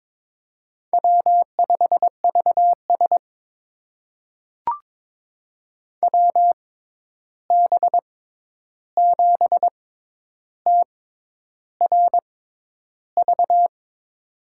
Morse code, heard as a keyed tone, about 22 wpm, spaced as for 10 wpm, 700 Hz. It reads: W5VS WB7TRV
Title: USA Call Signs - 6x Character Spacing - 22wpm